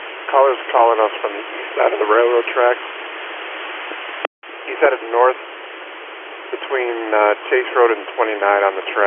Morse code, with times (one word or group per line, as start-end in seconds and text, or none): none